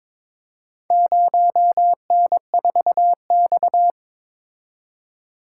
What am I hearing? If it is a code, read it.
Morse code, heard as a keyed tone, about 22 wpm, 700 Hz. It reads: 0N4X
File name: practice / 22wpm / Four Letter-Number Combinations - 22wpm